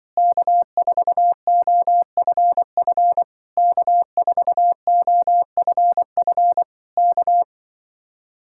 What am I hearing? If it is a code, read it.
Morse code, heard as a keyed tone, about 24 wpm, 700 Hz. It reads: K4OFF K4OFF K
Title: Speed-Racing Common QSO Phrases - 20wpm